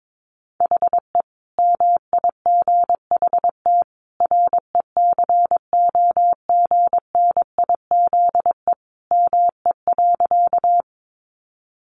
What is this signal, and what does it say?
Morse code, heard as a keyed tone, about 22 wpm, 700 Hz. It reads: HE MIGHT RECOGNIZE ME.
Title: Sentences from Top 1400 Words 22wpm